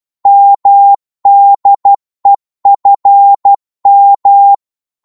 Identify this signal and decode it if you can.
Morse code, heard as a keyed tone, about 12 wpm, 800 Hz. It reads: MDEFM
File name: Morse / ltr-12wpm